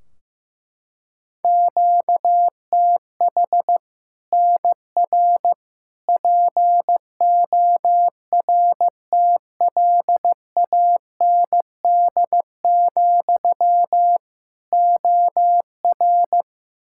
Morse code, heard as a keyed tone, about 15 wpm, 700 Hz. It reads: QTH NR PORTLAND, OR
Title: Practice QSO - Exchange 1 - Beginning - 15wpm